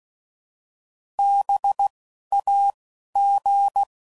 Morse code, heard as a keyed tone, about 16 wpm, 800 Hz. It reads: B A G